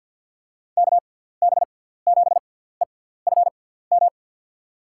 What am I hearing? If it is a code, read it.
Morse code, heard as a keyed tone, about 47 wpm, spaced as for 17 wpm, 700 Hz. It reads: KB7EFM
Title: USA Call Signs - 6 Characters - 6x Character Spacing - 50wpm